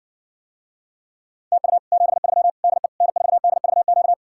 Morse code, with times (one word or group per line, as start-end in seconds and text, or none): 1.5-1.8 TU
1.9-2.5 73
2.6-2.9 DE
3.0-4.1 N4DVQ